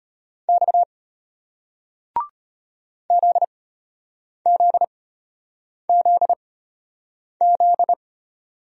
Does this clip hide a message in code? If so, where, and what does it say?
Yes: Morse code from 0.5 to 7.9 s, heard as a keyed tone, about 33 wpm, spaced as for 8 wpm, 700 Hz. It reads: X ZZZZ